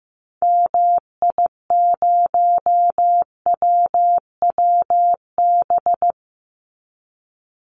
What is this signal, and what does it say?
Morse code, heard as a keyed tone, about 15 wpm, 700 Hz. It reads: MI0WWB
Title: Call Signs Repeated 2x - 15wpm